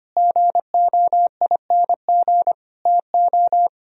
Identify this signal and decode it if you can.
Morse code, heard as a keyed tone, about 32 wpm, 700 Hz. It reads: GOING TO